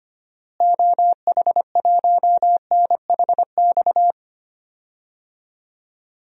Morse code, heard as a keyed tone, about 25 wpm, 700 Hz. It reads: OH1NHX
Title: DXCC-100 Call Signs - 6 Characters - Repeated 3x - 25wpm